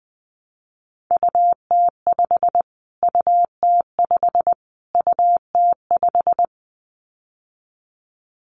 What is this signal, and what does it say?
Morse code, heard as a keyed tone, about 20 wpm, 700 Hz. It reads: UT5 UT5 UT5
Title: Callsign Prefixes for Top 25 DXCC - Repeated 3x - 20wpm